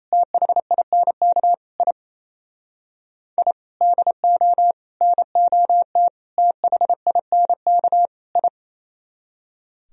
Morse code, heard as a keyed tone, about 33 wpm, 700 Hz. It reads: THINK I I DO NOT THINK I
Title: Speed-Racing - Sets of 5 Words 28wpm